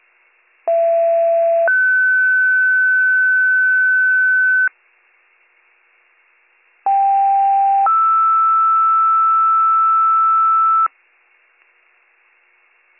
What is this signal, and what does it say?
Two-tone page, heard as a one-second tone, then a three-second tone, about 670 then 1500 Hz, about 790 then 1320 Hz. It is heard from 0.7 to 10.9 s.